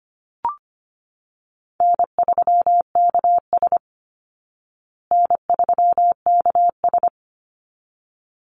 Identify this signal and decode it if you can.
Morse code, heard as a keyed tone, about 25 wpm, 700 Hz. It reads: N3KS N3KS
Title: Call Signs Repeated 3x - 25wpm